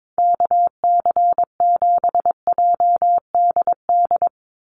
Morse code, heard as a keyed tone, about 22 wpm, 700 Hz. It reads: KC7JDD
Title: USA Call Signs 22wpm